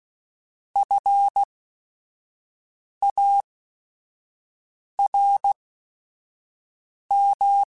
Morse code, heard as a keyed tone, about 16 wpm, spaced as for 5 wpm, 800 Hz. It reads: FARM
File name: General / General-mp3s / Houses @5